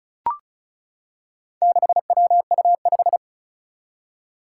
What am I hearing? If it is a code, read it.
Morse code, heard as a keyed tone, about 36 wpm, 700 Hz. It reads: 6WU5
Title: Four Challenging Letter-Number Combinations 35wpm